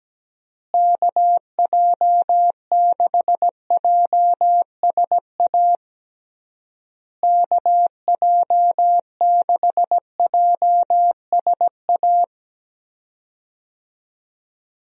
Morse code, heard as a keyed tone, about 17 wpm, 700 Hz. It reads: KJ6JSA KJ6JSA
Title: USA Call Signs - 6 Characters - Repeated 3x - 17wpm